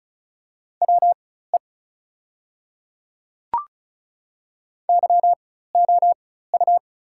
Morse code, heard as a keyed tone, about 35 wpm, spaced as for 16 wpm, 700 Hz. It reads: WE YOU
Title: 4x Character Spacing - Top 100 Words - 35wpm